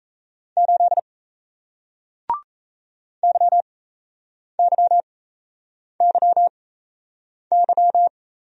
Morse code, heard as a keyed tone, about 38 wpm, spaced as for 8 wpm, 700 Hz. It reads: 8 YYYY